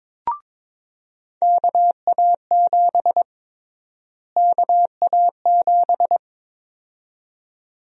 Morse code, heard as a keyed tone, about 22 wpm, 700 Hz. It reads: KA7 KA7
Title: USA Call Sign Prefixes - Repeated 2x - 22wpm